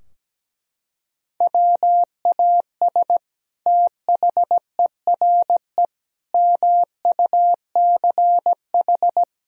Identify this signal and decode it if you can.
Morse code, heard as a keyed tone, about 17 wpm, 700 Hz. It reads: WAS THERE MUCH
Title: Sentences from Top 1000 Words - Review Part 4 - 17wpm